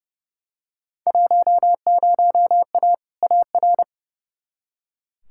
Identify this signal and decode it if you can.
Morse code, heard as a keyed tone, about 32 wpm, 700 Hz. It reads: 10A AR
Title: Field Day Exchanges Encore - 30wpm